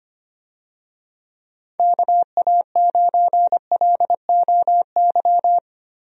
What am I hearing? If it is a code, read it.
Morse code, heard as a keyed tone, about 25 wpm, 700 Hz. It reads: KA9LOY
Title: USA Call Signs - 6 Characters - 25wpm